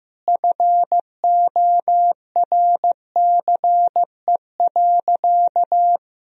Morse code, heard as a keyed tone, about 15 wpm, 700 Hz. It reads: FORCE.
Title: Sentences from Top 1000 Words - Review Part 3 - 15wpm